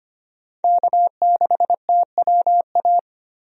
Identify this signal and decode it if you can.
Morse code, heard as a keyed tone, about 25 wpm, 700 Hz. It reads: K6TWA
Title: USA Call Signs - 5 Characters - 25wpm